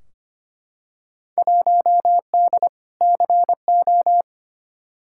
Morse code, heard as a keyed tone, about 25 wpm, 700 Hz. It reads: 1D CO
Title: Field Day Exchanges - 25wpm